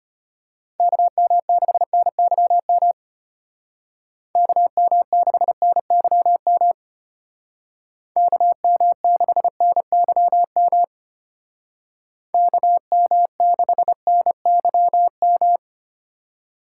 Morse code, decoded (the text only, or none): KM6NYM KM6NYM KM6NYM KM6NYM